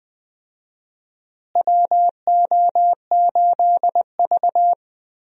Morse code, heard as a keyed tone, about 20 wpm, 700 Hz. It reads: WO8V
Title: USA Call Signs - 4 Characters - Repeated 2x - 20wpm